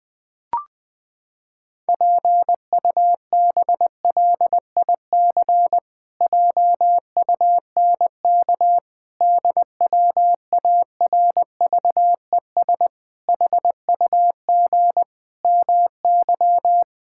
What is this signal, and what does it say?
Morse code, heard as a keyed tone, about 19 wpm, 700 Hz. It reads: PUBLIC JUNK DWARVES HUG MY
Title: Pangram Challenge v2 - 20wpm